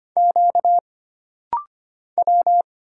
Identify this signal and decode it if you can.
Morse code, heard as a keyed tone, about 25 wpm, 700 Hz. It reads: Q W
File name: Single Character (Letter:Number:Punctuation:Prosign) - Rapid-Fire - 25wpm